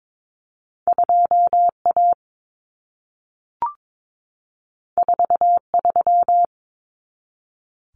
Morse code, heard as a keyed tone, about 22 wpm, 700 Hz. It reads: 2A 43
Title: Common Cut Numbers - 2-Digit 22wpm